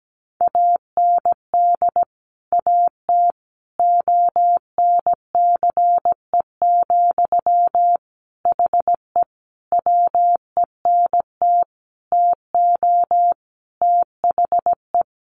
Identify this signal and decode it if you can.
Morse code, heard as a keyed tone, about 17 wpm, 700 Hz. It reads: AND AT ONCE, HE WENT TO THE
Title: Sentences from Top 1300 Words 17wpm